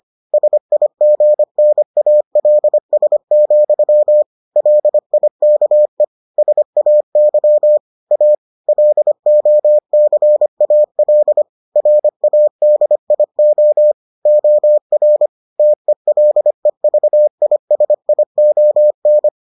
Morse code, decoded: SIGNALS, LIKE SAY A LOCAL RADIO OR TELEVISION

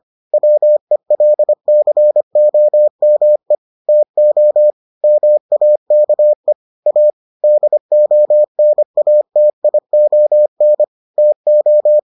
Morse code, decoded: WELCOME TO MAKE A DONATION TO

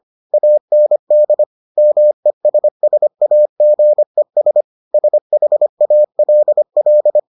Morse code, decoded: AND MESSAGES SHALL

